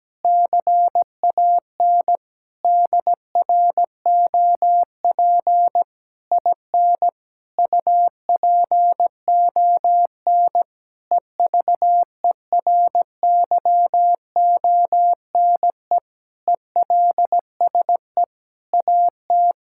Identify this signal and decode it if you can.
Morse code, heard as a keyed tone, about 17 wpm, 700 Hz. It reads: CAN DROP IN UPON EVERYONE ELSE AT
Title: Sentences from Top 1600 Words 17wpm